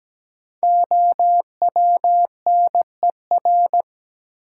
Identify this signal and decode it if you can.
Morse code, heard as a keyed tone, about 17 wpm, 700 Hz. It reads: OWNER
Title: Sentences from Top 1600 Words - New Word Drill Alt - 17wpm